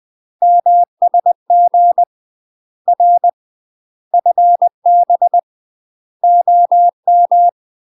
Morse code, heard as a keyed tone, about 20 wpm, 700 Hz. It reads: MSG R FB OM
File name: UK MASTER QSOs TRAINING / 20WPM / UK MASTER QSOs 0024